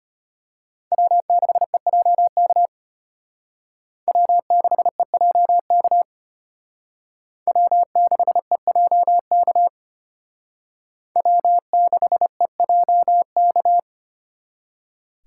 Morse code, heard as a keyed tone, about 32 wpm, 700 Hz. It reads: W6EJK W6EJK W6EJK W6EJK